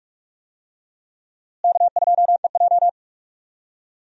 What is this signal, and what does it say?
Morse code, heard as a keyed tone, about 45 wpm, 700 Hz. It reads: K2EJ
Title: USA Call Signs - 4 Characters - Repeated 3x - 45wpm